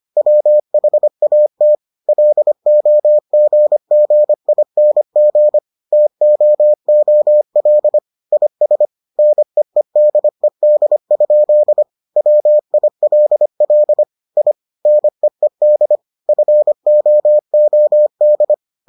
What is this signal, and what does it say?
Morse code, heard as a keyed tone, about 25 wpm, 600 Hz. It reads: WHAT LOGGING TOOL IS NEEDED? WILL I NEED FOOD